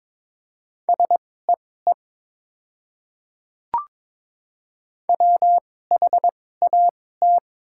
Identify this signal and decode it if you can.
Morse code, heard as a keyed tone, about 22 wpm, spaced as for 16 wpm, 700 Hz. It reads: SEE WHAT
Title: 2x Character Spacing - Top 100 Words - 22wpm